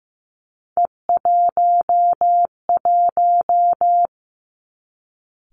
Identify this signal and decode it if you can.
Morse code, heard as a keyed tone, about 15 wpm, 700 Hz. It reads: E11
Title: Three Challenging Letter-Number Combinations 15wpm